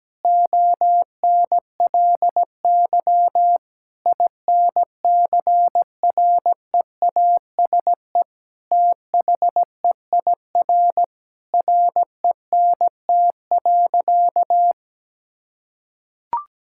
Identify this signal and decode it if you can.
Morse code, heard as a keyed tone, about 17 wpm, 700 Hz. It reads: ONLY INCREASE THEIR RENT.
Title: Sentences from Top 1000 Words - Review Part 5 - 17wpm